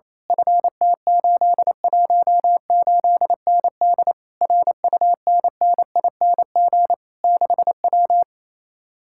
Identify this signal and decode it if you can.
Morse code, heard as a keyed tone, about 26 wpm, 700 Hz. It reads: FT818ND RUNNING 6W
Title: Speed-Racing Common QSO Phrases Encore - 28wpm